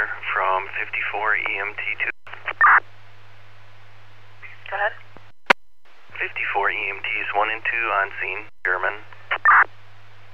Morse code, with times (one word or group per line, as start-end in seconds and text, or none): none